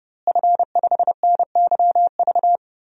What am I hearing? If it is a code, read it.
Morse code, heard as a keyed tone, about 32 wpm, 700 Hz. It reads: F5NYV
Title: DXCC-100 Call Signs - 5 Characters - Repeated 3x - 30wpm